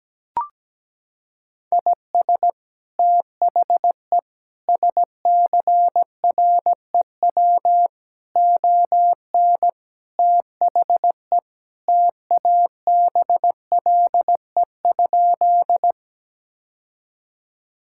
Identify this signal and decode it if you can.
Morse code, heard as a keyed tone, about 17 wpm, 700 Hz. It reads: IS THE SCREW ON THE TABLE?
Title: Sentences from Top 1600 Words 17wpm